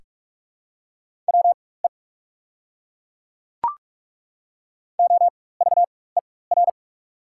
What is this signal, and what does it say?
Morse code, heard as a keyed tone, about 42 wpm, spaced as for 21 wpm, 690 Hz. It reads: WE OVER
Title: 4x Character Spacing - Top 200 Words - 45wpm